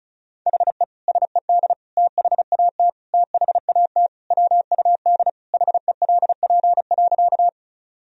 Morse code, heard as a keyed tone, about 34 wpm, 690 Hz. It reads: HE SED THAT THAT WUD HELP.
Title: Sentences from Top 200 Words and Abbreviations 35wpm